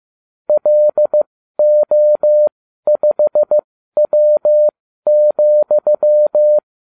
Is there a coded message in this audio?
Yes: Morse code from 0.5 to 6.6 s, heard as a keyed tone, about 15 wpm, spaced as for 12 wpm, 600 Hz. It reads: LO5W,